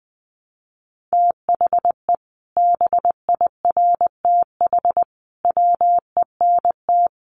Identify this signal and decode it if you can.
Morse code, heard as a keyed tone, about 20 wpm, 700 Hz. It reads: THE BIRTH WENT
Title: Sentences from Top 800 Words 20wpm